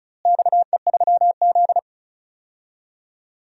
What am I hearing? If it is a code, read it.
Morse code, heard as a keyed tone, about 36 wpm, 700 Hz. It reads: XE3Z
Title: DXCC-100 Call Signs - 4 Characters - Repeated 2x - 35wpm